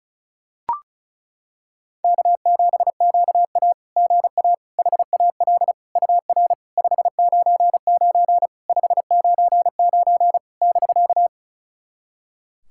Morse code, heard as a keyed tone, about 34 wpm, 700 Hz. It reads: K7QA GA HAL UR 599 599 <BK>
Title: SOTA QSO Practice 35wpm